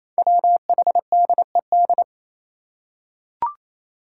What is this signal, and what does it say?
Morse code, heard as a keyed tone, about 27 wpm, 700 Hz. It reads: WHDED
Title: Five Challenging Letter-Number Combinations 28wpm